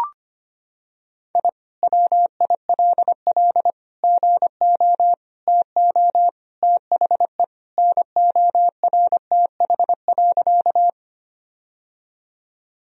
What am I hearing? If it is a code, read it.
Morse code, heard as a keyed tone, about 25 wpm, 700 Hz. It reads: I WILL GO TO THE NORTH.
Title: Sentences from Top 1000 Words - Review Part 1 - 25wpm